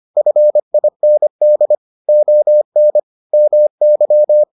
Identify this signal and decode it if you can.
Morse code, heard as a keyed tone, about 25 wpm, 600 Hz. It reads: FIND ON MY